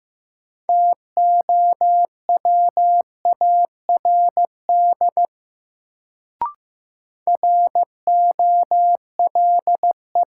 Morse code, heard as a keyed tone, about 15 wpm, 700 Hz. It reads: TOWARD ROLE
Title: Sentences from Top 1000 Words - Review Part 6 - New Word Drill - 15wpm